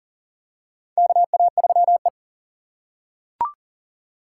Morse code, heard as a keyed tone, about 40 wpm, 700 Hz. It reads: KA3E
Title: USA Call Signs - 2x Character Spacing - 40wpm